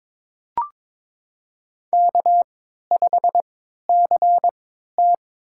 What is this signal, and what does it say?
Morse code, heard as a keyed tone, about 22 wpm, spaced as for 13 wpm, 700 Hz. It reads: K5CT
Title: USA Call Signs - 3x Character Spacing - 22wpm